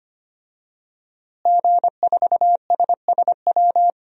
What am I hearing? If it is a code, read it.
Morse code, heard as a keyed tone, about 25 wpm, 700 Hz. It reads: G4SSW
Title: DXCC-100 Call Signs - 5 Characters - Repeated 3x - 25wpm